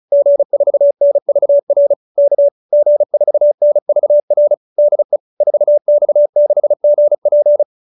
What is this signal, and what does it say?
Morse code, heard as a keyed tone, about 34 wpm, 590 Hz. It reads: G4NVR K G4NVR DE 4X6GP